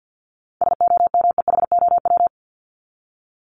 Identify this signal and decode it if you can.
Morse code, heard as a keyed tone, about 48 wpm, 700 Hz. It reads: SOMEHOW
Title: Top 1501-1750 Words 50wpm